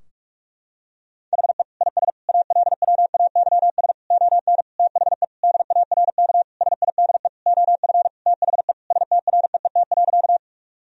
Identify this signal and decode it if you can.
Morse code, heard as a keyed tone, about 41 wpm, 700 Hz. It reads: HE IS ALWAYS ON THE DARK SIDE OF THE STREET.
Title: Sentences from Top 1000 Words - Review Part 3 - 45wpm